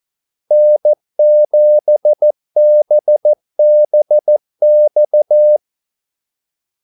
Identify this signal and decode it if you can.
Morse code, heard as a keyed tone, about 14 wpm, 600 Hz. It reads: N7BBX
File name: Callsigns-14wpm